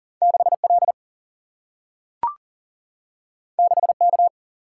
Morse code, heard as a keyed tone, about 40 wpm, 700 Hz. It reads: 6 L 6 K
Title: Single-Letter Single-Number 40wpm